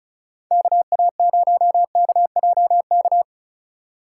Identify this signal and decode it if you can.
Morse code, heard as a keyed tone, about 34 wpm, 700 Hz. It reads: KA0KJK
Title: USA Call Signs - 6 Characters - 6x Character Spacing - 35wpm